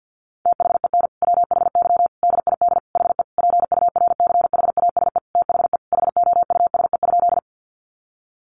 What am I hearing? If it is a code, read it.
Morse code, heard as a keyed tone, about 50 wpm, 700 Hz. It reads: THEN WHY DID HE PURCHASE THE HOUSE?
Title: Sentences from Top 800 Words 50wpm